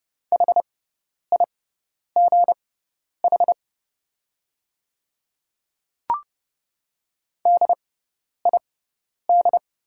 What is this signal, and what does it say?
Morse code, heard as a keyed tone, about 30 wpm, spaced as for 10 wpm, 700 Hz. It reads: HIGH DID
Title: 6x Character Spacing - Top 200 Words - 30wpm